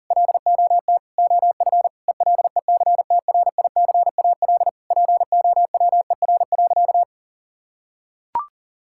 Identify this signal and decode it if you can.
Morse code, heard as a keyed tone, about 40 wpm, 700 Hz. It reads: LOT OF ELECTRICAL POWER.
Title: Sentences from Top 1000 Words - Review Part 6 - 40wpm